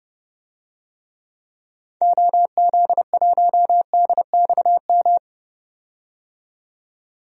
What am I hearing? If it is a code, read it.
Morse code, heard as a keyed tone, about 30 wpm, 700 Hz. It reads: OZ1DXM
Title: DXCC-100 Call Signs - 6 Characters - 30wpm